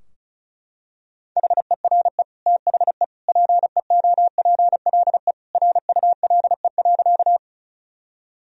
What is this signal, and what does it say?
Morse code, heard as a keyed tone, about 36 wpm, 700 Hz. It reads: HERE THE PEOPLE RULE.
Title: Sentences from Top 1000 Words - Review Part 2 - 35wpm